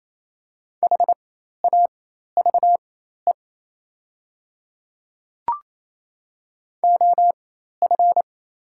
Morse code, heard as a keyed tone, about 27 wpm, spaced as for 13 wpm, 700 Hz. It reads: HAVE OF
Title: 4x Character Spacing - Top 300 Words - 28wpm